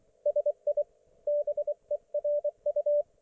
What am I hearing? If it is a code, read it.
Morse code, heard as a keyed tone, about 24 wpm, 590 Hz. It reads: SI BERU
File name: Archiv / CUC212 / ok1fhi cul212